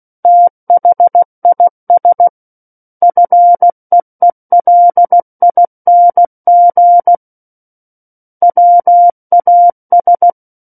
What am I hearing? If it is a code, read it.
Morse code, heard as a keyed tone, about 16 wpm, 700 Hz. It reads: THIS FEELING WAS
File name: Book One / 07-Chapter 7 - 16 WPM